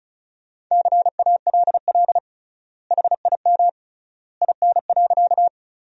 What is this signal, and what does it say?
Morse code, heard as a keyed tone, about 36 wpm, 700 Hz. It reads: CALL HIM IN.
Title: 3x Word-Spacing - Sentences from Top 100 Words - 35wpm